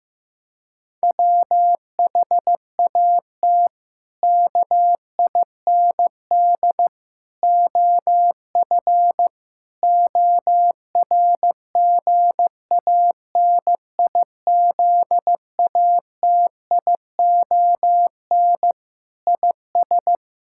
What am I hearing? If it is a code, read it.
Morse code, heard as a keyed tone, about 16 wpm, 700 Hz. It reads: WHAT KIND OF ORGANIZATION IS
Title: Sentences from Top 1000 Words - Review Part 5 - 15wpm